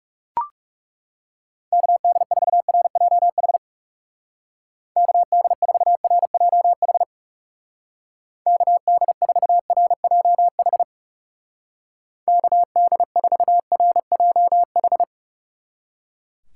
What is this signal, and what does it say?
Morse code, heard as a keyed tone, about 36 wpm, 700 Hz. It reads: KD4RJH KD4RJH KD4RJH KD4RJH
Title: Speed Racing - USA 2x3 Call Signs 30wpm